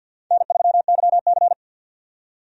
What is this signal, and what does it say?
Morse code, heard as a keyed tone, about 53 wpm, 700 Hz. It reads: N3YC
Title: USA Call Signs - 4 Characters - 50wpm